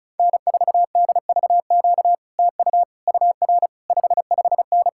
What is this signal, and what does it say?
Morse code, heard as a keyed tone, about 40 wpm, 700 Hz. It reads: N4DVQ TU UR 55N